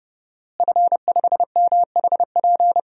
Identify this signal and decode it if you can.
Morse code, heard as a keyed tone, about 30 wpm, 700 Hz. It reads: F5MHP
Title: DXCC-100 Call Signs - 5 Characters - Repeated 2x - 30wpm